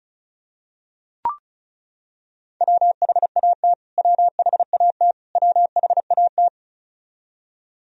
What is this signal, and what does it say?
Morse code, heard as a keyed tone, about 34 wpm, 700 Hz. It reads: WHAT WHAT WHAT
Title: Top 100 Words Repeated 35wpm